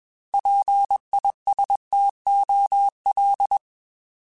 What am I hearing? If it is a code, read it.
Morse code, heard as a keyed tone, about 21 wpm, 800 Hz. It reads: PISTOL
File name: Alphabetical / Alphabetical mp3s / Ps @20